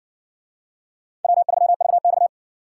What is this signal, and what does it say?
Morse code, heard as a keyed tone, about 55 wpm, 700 Hz. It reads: W3VX